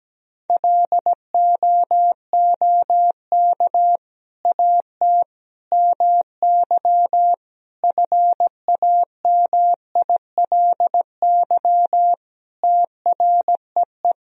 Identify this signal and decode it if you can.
Morse code, heard as a keyed tone, about 17 wpm, 700 Hz. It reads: LOOK AT MY FAMILY TREE
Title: Sentences from Top 1000 Words - Review Part 1 - 17wpm